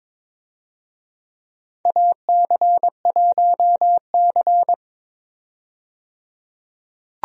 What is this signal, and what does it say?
Morse code, heard as a keyed tone, about 23 wpm, 700 Hz. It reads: AC1C
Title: USA Call Signs - 4 Characters - 22wpm